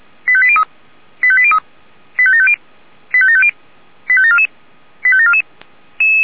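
Five-tone selective call (ZVEI), heard as quick runs of five tones, tones of about 70 ms. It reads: 85803 85803 85750 85750 85744 85744
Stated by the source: ZVEI five-tone selective call received by SDR.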